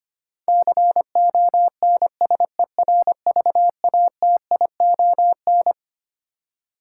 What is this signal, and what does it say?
Morse code, heard as a keyed tone, about 25 wpm, 700 Hz. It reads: CONSERVATION
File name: Twelve Letter Words - 25wpm